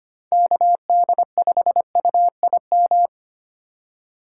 Morse code, heard as a keyed tone, about 25 wpm, 700 Hz. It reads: KD5UIM
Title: USA Call Signs - 6 Characters - 3x Character Spacing - 25wpm